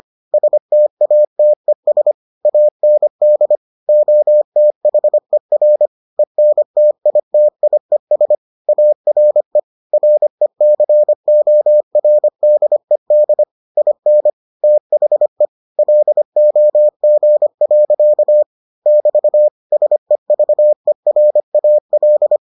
Morse code, heard as a keyed tone, about 25 wpm, 600 Hz. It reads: STATES AND OTHER ENTITIES ARE RECORDED IN THE LOG. <BT> SEVERAL